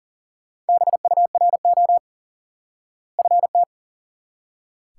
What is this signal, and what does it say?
Morse code, heard as a keyed tone, about 37 wpm, 700 Hz. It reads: BURO FT
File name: IQE Course 3 - Lesson 89 - YAGI - Two Elements - 40wpm